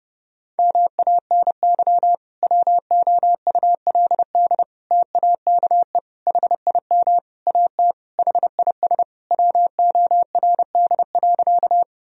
Morse code, heard as a keyed tone, about 30 wpm, 700 Hz. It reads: MANY WOULD TAKE HIM AT HIS WORD.
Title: Sentences from Top 1000 Words - Review Part 1 - 30wpm